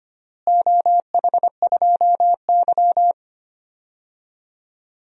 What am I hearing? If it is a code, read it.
Morse code, heard as a keyed tone, about 25 wpm, 700 Hz. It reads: OH2Y
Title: DXCC-100 Call Signs - 4 Characters - Repeated 2x - 25wpm